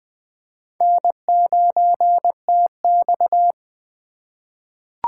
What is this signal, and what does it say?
Morse code, heard as a keyed tone, about 20 wpm, 700 Hz. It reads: N9TX
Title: USA Call Signs - 2x Character Spacing - 20wpm